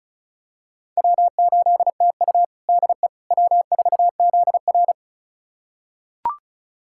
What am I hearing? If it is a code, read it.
Morse code, heard as a keyed tone, about 36 wpm, 700 Hz. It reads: W8TU DE W4ZR
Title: USA Callsign DE Callsign - 4 Characters - 35wpm